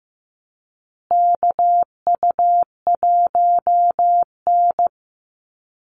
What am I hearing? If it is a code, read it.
Morse code, heard as a keyed tone, about 14 wpm, 700 Hz. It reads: KU1N
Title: Call Signs Repeated 2x - 15wpm